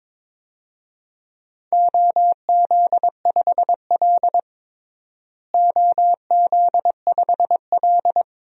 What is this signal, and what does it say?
Morse code, heard as a keyed tone, about 22 wpm, 700 Hz. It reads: OZ5L OZ5L
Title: DXCC-100 Call Signs - 4 Characters - Repeated 2x - 22wpm